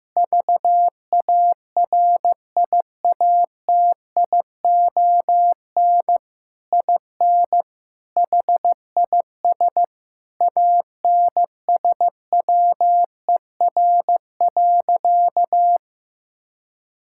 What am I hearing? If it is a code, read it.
Morse code, heard as a keyed tone, about 16 wpm, 700 Hz. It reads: VARIATION IN HIS ANSWER.
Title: Sentences from Top 1600 Words 15wpm